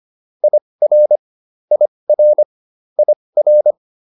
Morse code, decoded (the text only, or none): IR IR IR